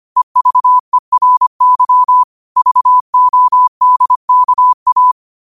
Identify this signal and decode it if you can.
Morse code, heard as a keyed tone, about 25 wpm, 1000 Hz. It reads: EVERY VODKA
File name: qotdfiles / 25wpm